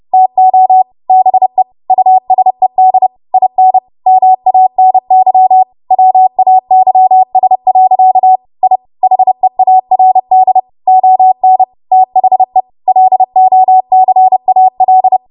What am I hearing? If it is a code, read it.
Morse code, heard as a keyed tone, about 30 wpm, 750 Hz. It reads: TO BE USED IN MANY WAYS. I HEARD ON THE LOCAL
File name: Morse / 30WPM